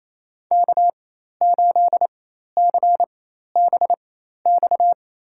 Morse code, heard as a keyed tone, about 27 wpm, spaced as for 13 wpm, 700 Hz. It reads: K8CBX
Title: USA Call Signs - 4x Character Spacing - 28wpm